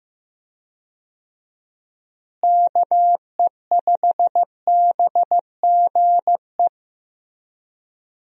Morse code, heard as a keyed tone, about 15 wpm, 700 Hz. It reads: KE5BGE
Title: USA Call Signs - 6 Characters - Repeated 2x - 15wpm